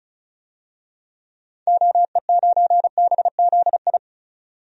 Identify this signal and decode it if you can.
Morse code, heard as a keyed tone, about 35 wpm, 700 Hz. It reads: OE9BZI
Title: DXCC-100 Call Signs - 6 Characters - 35wpm